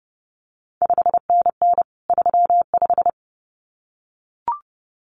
Morse code, heard as a keyed tone, about 30 wpm, 700 Hz. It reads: 5NN 35
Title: CQ WW DX Contest Exchange Encore - 30wpm